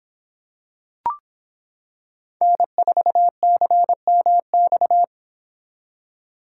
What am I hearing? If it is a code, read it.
Morse code, heard as a keyed tone, about 25 wpm, 700 Hz. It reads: N4CMX